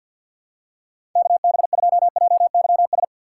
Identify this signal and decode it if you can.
Morse code, heard as a keyed tone, about 47 wpm, 700 Hz. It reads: KB2JYS
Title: USA Call Signs - 6 Characters - 4x Character Spacing - 50wpm